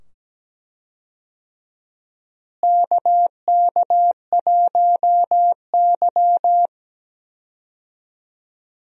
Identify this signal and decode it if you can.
Morse code, heard as a keyed tone, about 17 wpm, 700 Hz. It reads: KK1Y
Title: USA Call Signs - 4 Characters - Repeated 2x - 17wpm